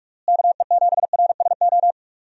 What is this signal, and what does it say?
Morse code, heard as a keyed tone, about 47 wpm, 700 Hz. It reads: KE7RSO
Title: USA Call Signs - 6 Characters - Repeated 3x - 45wpm